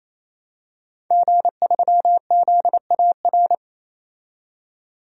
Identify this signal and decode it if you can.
Morse code, heard as a keyed tone, about 28 wpm, 700 Hz. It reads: G3ZAR
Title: DXCC-100 Call Signs - 5 Characters - 28wpm